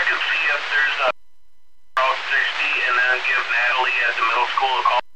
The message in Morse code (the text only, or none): none